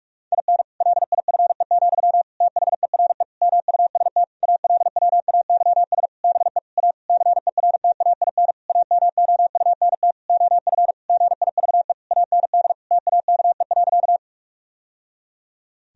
Morse code, heard as a keyed tone, about 46 wpm, 700 Hz. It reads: IN LIFE, THERE MUST ALWAYS BE A CERTAIN AMOUNT OF GIVE AND TAKE.
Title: Sentences from Top 1000 Words - Review Part 4 - 45wpm